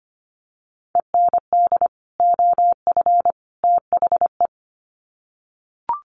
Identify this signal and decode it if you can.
Morse code, heard as a keyed tone, about 25 wpm, 700 Hz. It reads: END OF THE
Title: Sets of 3 Words 25wpm